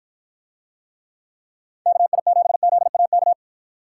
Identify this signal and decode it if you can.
Morse code, heard as a keyed tone, about 53 wpm, 700 Hz. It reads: KI7ZAK